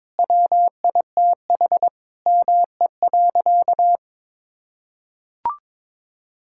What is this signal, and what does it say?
Morse code, heard as a keyed tone, about 22 wpm, 700 Hz. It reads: WITH ME.
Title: Sentences from Top 1000 Words - Review Part 6 - 22wpm